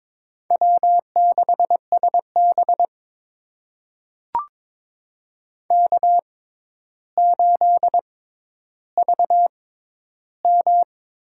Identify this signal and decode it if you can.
Morse code, heard as a keyed tone, about 22 wpm, spaced as for 11 wpm, 700 Hz. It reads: W6SB K8VM
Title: USA Call Signs - 4 Characters - 6x Character Spacing - 22wpm